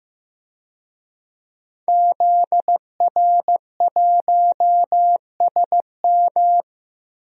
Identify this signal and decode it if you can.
Morse code, heard as a keyed tone, about 15 wpm, 700 Hz. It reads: ZR1SM
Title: DXCC-100 Call Signs - 5 Characters - Repeated 3x - 15wpm